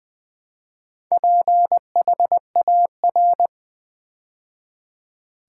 Morse code, heard as a keyed tone, about 20 wpm, 700 Hz. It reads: PHAR
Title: Build A Word - 2 Letters At A Time - 8-Letter Words - 20wpm